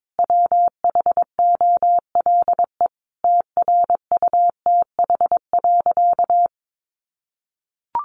Morse code, heard as a keyed tone, about 22 wpm, 700 Hz. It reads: WHOLE TRUTH.
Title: Sentences from Top 800 Words 22wpm